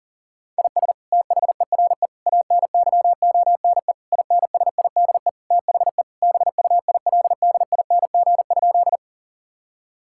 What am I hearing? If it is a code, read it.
Morse code, heard as a keyed tone, about 37 wpm, 700 Hz. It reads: IS THERE ANYONE INSIDE THE BUILDING?
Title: Sentences from Top 1000 Words - Review Part 6 - 40wpm